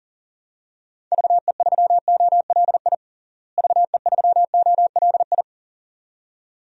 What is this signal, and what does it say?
Morse code, heard as a keyed tone, about 40 wpm, 700 Hz. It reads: VE3OLI VE3OLI
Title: DXCC-100 Call Signs - 6 Characters - Repeated 2x - 40wpm